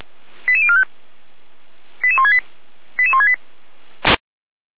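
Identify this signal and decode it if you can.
Five-tone selective call (ZVEI), heard as quick runs of five tones, tones of about 70 ms. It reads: 88045 88168 88168
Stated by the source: ZVEI five-tone selective call received with an SDR, from feuerwehr-orlamuende.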